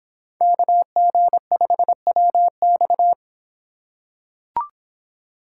Morse code, heard as a keyed tone, about 23 wpm, 700 Hz. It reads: KG5WX